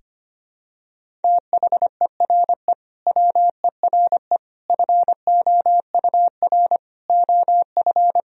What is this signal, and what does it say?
Morse code, heard as a keyed tone, about 25 wpm, 700 Hz. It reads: THERE WERE FOUR OF
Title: Sentences from Top 1000 Words - Review Part 1 - 25wpm